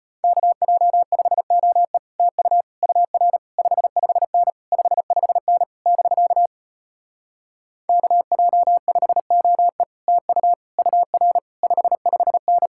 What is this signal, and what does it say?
Morse code, heard as a keyed tone, about 33 wpm, 700 Hz. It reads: KJ5OE TU UR 55N 55N <BK> KJ5OE TU UR 55N